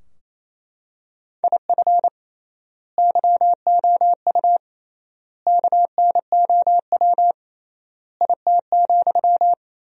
Morse code, heard as a keyed tone, about 28 wpm, 700 Hz. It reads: IF YOU KNOW IT,
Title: 3x Word-Spacing - Sentences from Top 400 Words - 28wpm